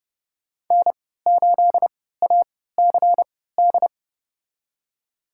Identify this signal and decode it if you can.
Morse code, heard as a keyed tone, about 30 wpm, spaced as for 17 wpm, 700 Hz. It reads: N8ACD
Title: USA Call Signs - 3x Character Spacing - 30wpm